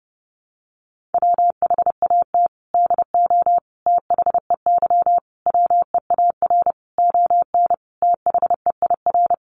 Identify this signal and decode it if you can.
Morse code, heard as a keyed tone, about 30 wpm, 700 Hz. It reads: WHAT DO THEY WEAR ON THEIR
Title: Sentences from Top 1500 Words 30wpm